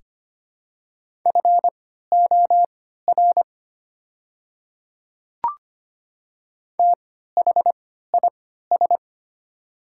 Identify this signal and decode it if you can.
Morse code, heard as a keyed tone, about 25 wpm, spaced as for 14 wpm, 700 Hz. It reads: FOR THIS
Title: 3x Character Spacing - Top 100 Words - 25wpm